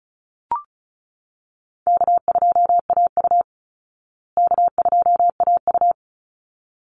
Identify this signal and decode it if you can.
Morse code, heard as a keyed tone, about 34 wpm, 700 Hz. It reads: K2AU K2AU
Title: Call Signs Repeated 3x - 35wpm